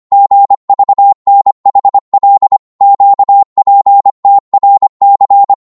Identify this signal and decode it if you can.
Morse code, heard as a keyed tone, about 25 wpm, 800 Hz. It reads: GVNHL QPTRC